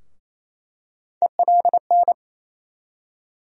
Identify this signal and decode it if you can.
Morse code, heard as a keyed tone, about 28 wpm, 700 Hz. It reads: ELN
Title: USA Call Sign Postfixes - 3 Letters 28wpm